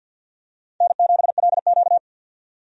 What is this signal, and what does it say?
Morse code, heard as a keyed tone, about 48 wpm, 700 Hz. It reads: N7LQ
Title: USA Call Signs - 4 Characters - 50wpm